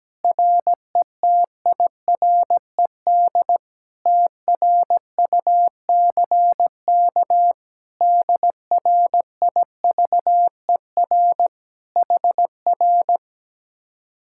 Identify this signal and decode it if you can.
Morse code, heard as a keyed tone, about 17 wpm, 700 Hz. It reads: RETIRED TRUCK DRIVER HR
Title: Speed-Racing Common QSO Phrases Encore - 17wpm